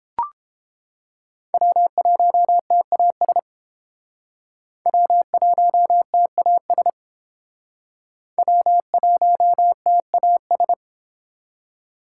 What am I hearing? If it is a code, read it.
Morse code, heard as a keyed tone, about 31 wpm, 700 Hz. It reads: W1TAS W1TAS W1TAS